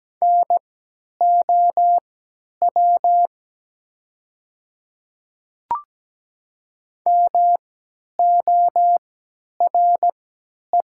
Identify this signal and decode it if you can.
Morse code, heard as a keyed tone, about 17 wpm, spaced as for 10 wpm, 700 Hz. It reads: NOW MORE
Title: 3x Character Spacing - Top 200 Words - 17wpm